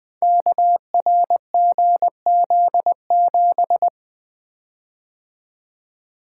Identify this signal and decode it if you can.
Morse code, heard as a keyed tone, about 20 wpm, 700 Hz. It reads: KRGZ7
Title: Code Groups - 5 Letter:Numbers - 20wpm